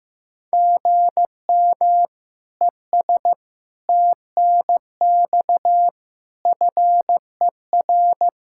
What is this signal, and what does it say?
Morse code, heard as a keyed tone, about 16 wpm, 700 Hz. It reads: GM ES TNX FER
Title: K1USN SST QSO Practice 15wpm